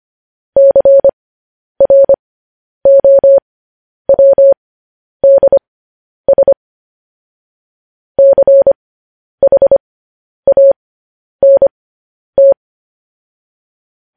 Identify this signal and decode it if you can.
Morse code, heard as a keyed tone, about 25 wpm, spaced as for 10 wpm, 550 Hz. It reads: CROWDS CHANT